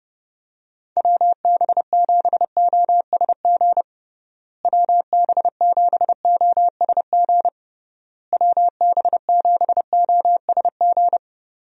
Morse code, heard as a keyed tone, about 31 wpm, 700 Hz. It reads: WB7OSG WB7OSG WB7OSG
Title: USA Call Signs - 6 Characters - Repeated 3x - 30wpm